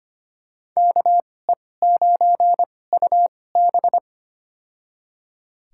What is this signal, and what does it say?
Morse code, heard as a keyed tone, about 25 wpm, spaced as for 18 wpm, 700 Hz. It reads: KE9UB
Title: USA Call Signs - 5 Characters - 2x Character Spacing - 25wpm